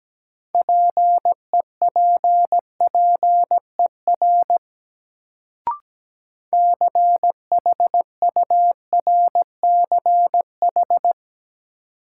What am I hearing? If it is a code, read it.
Morse code, heard as a keyed tone, about 17 wpm, 700 Hz. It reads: PEPPER CHURCH
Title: Sets of 5 Words - New Word Drill - 17wpm